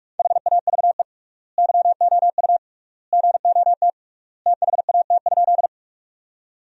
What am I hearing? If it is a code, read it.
Morse code, heard as a keyed tone, about 42 wpm, 700 Hz. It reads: HAVE YOU GOT THAT?
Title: 3x Word-Spacing - Sentences from Top 300 Words - 45wpm